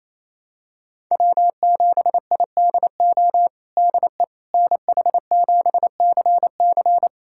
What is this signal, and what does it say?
Morse code, heard as a keyed tone, about 28 wpm, 700 Hz. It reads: W7IDO DE NH7CC
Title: USA Callsign DE Callsign - 5 Characters - 28wpm